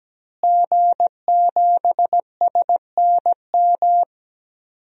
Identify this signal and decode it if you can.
Morse code, heard as a keyed tone, about 17 wpm, 700 Hz. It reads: G7SNM
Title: DXCC-100 Call Signs - 5 Characters - Repeated 2x - 17wpm